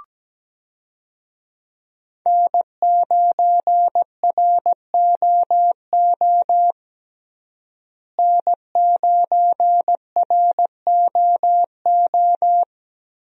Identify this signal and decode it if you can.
Morse code, heard as a keyed tone, about 17 wpm, 700 Hz. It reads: N9ROO N9ROO
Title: USA Call Signs - 5 Characters - Repeated 3x - 17wpm